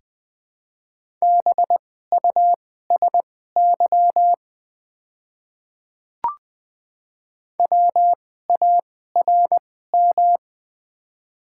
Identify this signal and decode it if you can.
Morse code, heard as a keyed tone, about 20 wpm, spaced as for 14 wpm, 690 Hz. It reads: BUSY WARM
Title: 2x Character Spacing - Top 500 Words - 20wpm